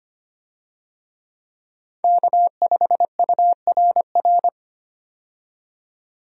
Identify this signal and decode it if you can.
Morse code, heard as a keyed tone, about 25 wpm, 700 Hz. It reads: K5URR
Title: USA Call Signs - 5 Characters - 25wpm